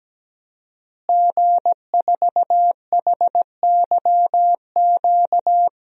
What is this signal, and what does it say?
Morse code, heard as a keyed tone, about 17 wpm, 700 Hz. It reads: G4HYQ
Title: DXCC-100 Call Signs - 5 Characters - Repeated 2x - 17wpm